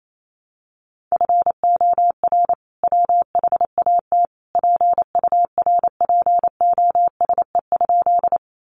Morse code, heard as a keyed tone, about 28 wpm, 700 Hz. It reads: FOR WHAT PURPOSE?
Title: Sentences from Top 800 Words 28wpm